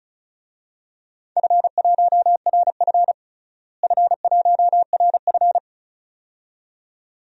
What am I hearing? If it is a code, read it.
Morse code, heard as a keyed tone, about 36 wpm, 700 Hz. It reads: F1RF F1RF
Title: DXCC-100 Call Signs - 4 Characters - Repeated 2x - 35wpm